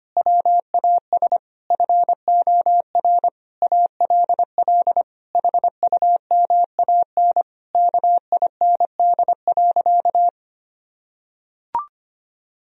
Morse code, heard as a keyed tone, about 25 wpm, 700 Hz. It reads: WAS FOR ALL HUMAN KIND.
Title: Sentences from Top 1000 Words - Review Part 4 - 25wpm